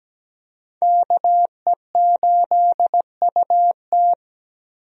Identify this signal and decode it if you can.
Morse code, heard as a keyed tone, about 17 wpm, 700 Hz. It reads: KE8UT
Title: USA Call Signs - 5 Characters - 17wpm